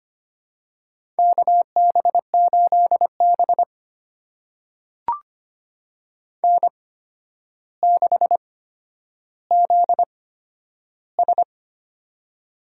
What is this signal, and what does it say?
Morse code, heard as a keyed tone, about 25 wpm, spaced as for 11 wpm, 700 Hz. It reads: KB8B N6ZS
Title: USA Call Signs - 4 Characters - 8x Character Spacing - 25wpm